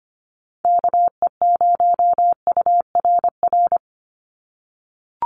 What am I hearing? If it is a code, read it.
Morse code, heard as a keyed tone, about 25 wpm, 700 Hz. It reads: KE0URR
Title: USA Call Signs 25wpm